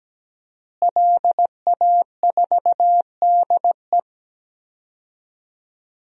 Morse code, heard as a keyed tone, about 17 wpm, 700 Hz. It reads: LA4DE
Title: DXCC-100 Call Signs - 5 Characters - Repeated 2x - 17wpm